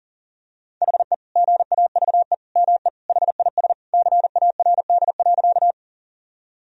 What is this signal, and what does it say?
Morse code, heard as a keyed tone, about 40 wpm, 700 Hz. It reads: HE GAVE ME HIS CARD.
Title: Sentences from Top 1000 Words - Review Part 4 - 40wpm